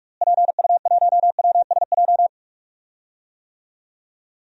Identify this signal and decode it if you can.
Morse code, heard as a keyed tone, about 44 wpm, 700 Hz. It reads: PU1WSJ
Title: DXCC-100 Call Signs - 6 Characters - Repeated 3x - 45wpm